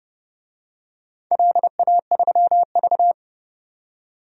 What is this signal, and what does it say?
Morse code, heard as a keyed tone, about 31 wpm, 700 Hz. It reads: LA3V
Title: DXCC-100 Call Signs - 4 Characters - 30wpm